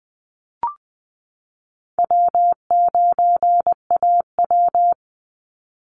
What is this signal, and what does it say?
Morse code, heard as a keyed tone, about 20 wpm, 700 Hz. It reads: W9AW
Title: Call Signs - 20wpm